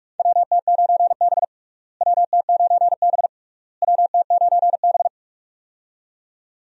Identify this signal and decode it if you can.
Morse code, heard as a keyed tone, about 46 wpm, 700 Hz. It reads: WT9B WT9B WT9B
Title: USA Call Signs - 4 Characters - Repeated 3x - 45wpm